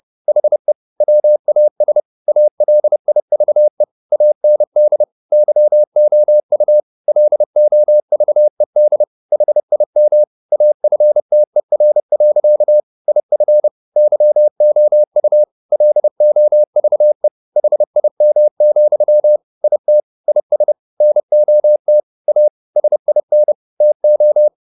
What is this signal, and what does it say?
Morse code, heard as a keyed tone, about 30 wpm, 600 Hz. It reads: HE WAS ALIVE AND YOU LOVED HIM AFTER. IF YOU LOVE HIM, IT IS NOT A SIN TO